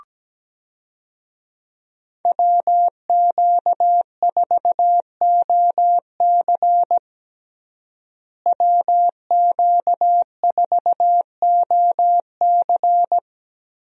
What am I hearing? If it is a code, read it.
Morse code, heard as a keyed tone, about 17 wpm, 700 Hz. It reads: WQ4OC WQ4OC
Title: USA Call Signs - 5 Characters - Repeated 3x - 17wpm